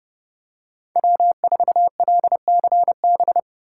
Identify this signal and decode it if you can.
Morse code, heard as a keyed tone, about 29 wpm, 700 Hz. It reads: W4LCB